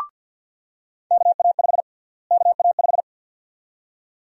Morse code, heard as a keyed tone, about 49 wpm, 700 Hz. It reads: KA5 KA5
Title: USA Call Sign Prefixes - Repeated 2x - 50wpm